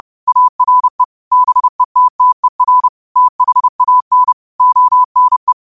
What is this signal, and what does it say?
Morse code, heard as a keyed tone, about 30 wpm, 1000 Hz. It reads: ARE BETTER THAN ONE
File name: 30wpm